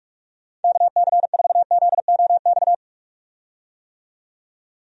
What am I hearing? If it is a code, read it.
Morse code, heard as a keyed tone, about 46 wpm, 700 Hz. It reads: KC4ZOX
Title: USA Call Signs - 6 Characters - 45wpm